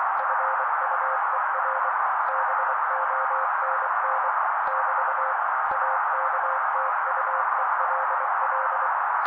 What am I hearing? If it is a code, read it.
Morse code, heard as a keyed tone, about 24 wpm, 530 Hz. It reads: FUER BONN <BT> AKTUELL
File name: daily / 23wpm noise